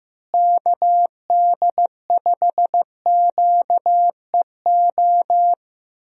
Morse code, heard as a keyed tone, about 16 wpm, 700 Hz. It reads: KD5QEO
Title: USA Call Signs - 6 Characters - Repeated 3x - 15wpm